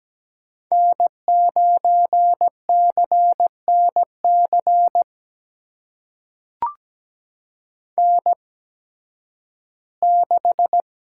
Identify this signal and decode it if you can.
Morse code, heard as a keyed tone, about 17 wpm, spaced as for 11 wpm, 700 Hz. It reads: N9CNC N6
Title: USA Call Signs - 5 Characters - 8x Character Spacing - 17wpm